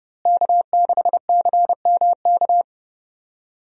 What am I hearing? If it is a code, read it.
Morse code, heard as a keyed tone, about 31 wpm, 700 Hz. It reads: K6CMK
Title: USA Call Signs - 5 Characters - Repeated 3x - 30wpm